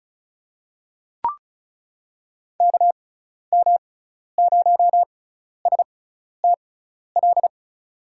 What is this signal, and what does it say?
Morse code, heard as a keyed tone, about 34 wpm, spaced as for 12 wpm, 700 Hz. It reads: KM0STL
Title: USA Call Signs - 6x Character Spacing - 35wpm